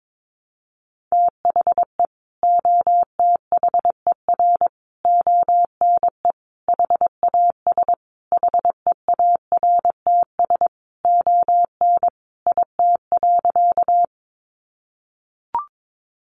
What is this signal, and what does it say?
Morse code, heard as a keyed tone, about 22 wpm, 700 Hz. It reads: THE OTHER ONE HAS HEARTS ON IT.
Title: Additional Easy Phrases 22wpm